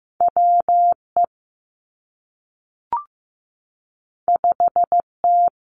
Morse code, heard as a keyed tone, about 15 wpm, 700 Hz. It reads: W E 5 T
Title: Two Challenging Letter-Number Combinations 15wpm